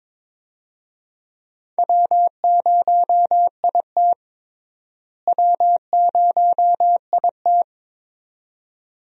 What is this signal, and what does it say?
Morse code, heard as a keyed tone, about 21 wpm, 700 Hz. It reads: W0IT W0IT
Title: USA Call Signs - 4 Characters - Repeated 2x - 22wpm